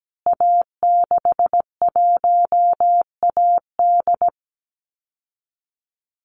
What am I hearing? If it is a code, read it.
Morse code, heard as a keyed tone, about 17 wpm, 700 Hz. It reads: A61AD
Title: Call Signs Repeated 2x - 17wpm